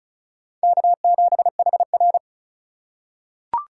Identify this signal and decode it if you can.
Morse code, heard as a keyed tone, about 36 wpm, 700 Hz. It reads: K7HR
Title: USA Call Signs - 4x Character Spacing - 35wpm